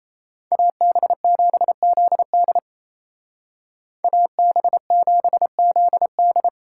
Morse code, heard as a keyed tone, about 36 wpm, 700 Hz. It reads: AB7ZD AB7ZD